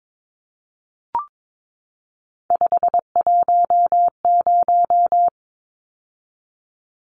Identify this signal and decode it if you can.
Morse code, heard as a keyed tone, about 22 wpm, 700 Hz. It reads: 510
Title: Numbers - 3-Digit 22wpm